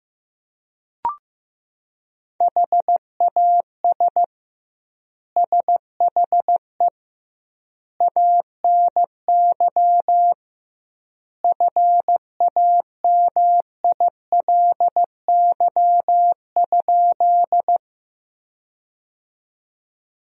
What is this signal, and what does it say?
Morse code, heard as a keyed tone, about 15 wpm, 700 Hz. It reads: HAS SHE ANY FAMILY?
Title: 2x Word-Spacing - Sentences from Top 400 Words - 15wpm